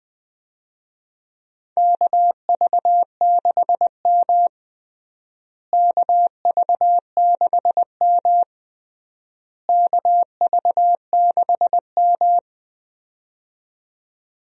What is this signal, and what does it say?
Morse code, heard as a keyed tone, about 20 wpm, 700 Hz. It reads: KV6M KV6M KV6M
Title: USA Call Signs - 4 Characters - Repeated 3x - 20wpm